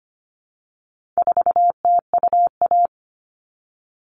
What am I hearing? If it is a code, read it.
Morse code, heard as a keyed tone, about 24 wpm, 700 Hz. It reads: 4TUA
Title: All Cut Numbers - 4-Digit 25wpm